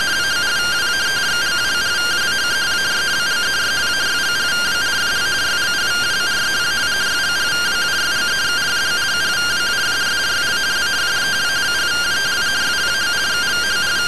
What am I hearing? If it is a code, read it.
Two-tone teleprinter signal, about 170 Hz shift.
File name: findings / CT AXIAL -111.gen-11 rtty rx